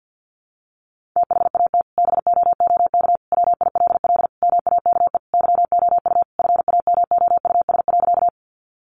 Morse code, heard as a keyed tone, about 50 wpm, 700 Hz. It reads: THAT BOOK WILL MAKE YOU FAMOUS.
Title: Sentences from Top 1500 Words 50wpm